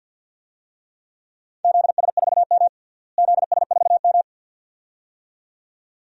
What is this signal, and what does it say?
Morse code, heard as a keyed tone, about 52 wpm, 700 Hz. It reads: ZS4M ZS4M
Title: DXCC-100 Call Signs - 4 Characters - Repeated 2x - 50wpm